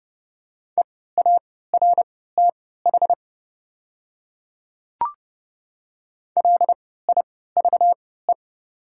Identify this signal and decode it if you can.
Morse code, heard as a keyed tone, about 30 wpm, spaced as for 17 wpm, 700 Hz. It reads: EARTH LIVE
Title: 3x Character Spacing - Top 200 Words - 30wpm